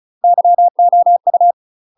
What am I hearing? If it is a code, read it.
Morse code, heard as a keyed tone, about 35 wpm, 700 Hz. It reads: YOU
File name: UK MASTER QSOs TRAINING / 35WPM / UK MASTER QSOs 0035